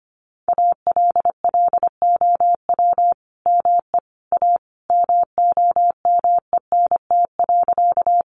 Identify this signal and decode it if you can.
Morse code, heard as a keyed tone, about 25 wpm, 700 Hz. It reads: ALLOW ME A MOMENT.
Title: Sentences from Top 1000 Words 25wpm